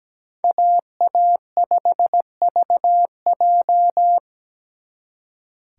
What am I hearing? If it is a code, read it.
Morse code, heard as a keyed tone, about 20 wpm, 700 Hz. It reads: AA5VJ
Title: Speed Racing - USA 2x2 Call Signs 17wpm